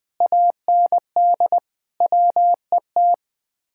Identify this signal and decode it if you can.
Morse code, heard as a keyed tone, about 20 wpm, 700 Hz. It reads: AND WET
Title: Speed-Racing Common QSO Phrases - 17wpm